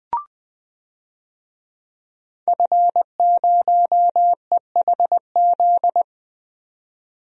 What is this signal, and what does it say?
Morse code, heard as a keyed tone, about 20 wpm, 700 Hz. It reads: F0EHZ
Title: DXCC-100 Call Signs - 5 Characters - 20wpm